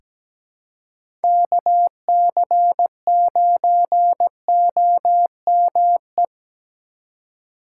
Morse code, heard as a keyed tone, about 17 wpm, 700 Hz. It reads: KC9OME
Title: USA Call Signs - 4x Character Spacing - 17wpm